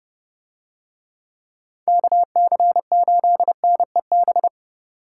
Morse code, heard as a keyed tone, about 30 wpm, 700 Hz. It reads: KC8NEB